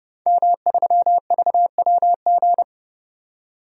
Morse code, heard as a keyed tone, about 30 wpm, 700 Hz. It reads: M3VWG
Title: DXCC-100 Call Signs - 5 Characters - 30wpm